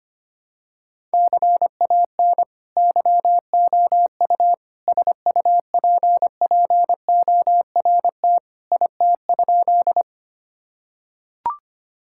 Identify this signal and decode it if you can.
Morse code, heard as a keyed tone, about 25 wpm, 700 Hz. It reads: CAN YOU SUPPORT IT?
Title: Sentences from Top 1000 Words - Review Part 5 - 25wpm